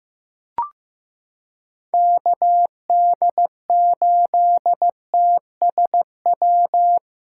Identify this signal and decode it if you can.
Morse code, heard as a keyed tone, about 16 wpm, 700 Hz. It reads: KD8TSW
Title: USA Callsign DE Callsign - 6 Characters - 15wpm